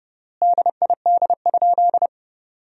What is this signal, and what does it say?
Morse code, heard as a keyed tone, about 30 wpm, 700 Hz. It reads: DID?
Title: 3x Word-Spacing - Sentences from Top 100 Words - 30wpm